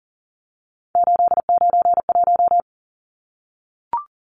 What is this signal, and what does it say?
Morse code, heard as a keyed tone, about 40 wpm, 700 Hz. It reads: 891
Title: Numbers - 3-Digit 40wpm